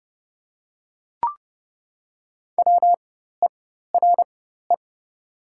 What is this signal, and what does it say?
Morse code, heard as a keyed tone, about 31 wpm, spaced as for 14 wpm, 700 Hz. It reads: WERE
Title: 4x Character Spacing - Top 300 Words - 30wpm